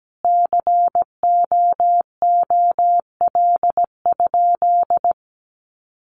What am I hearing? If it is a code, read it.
Morse code, heard as a keyed tone, about 17 wpm, 700 Hz. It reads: COOL?
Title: Sentences from Top 1200 Words 17wpm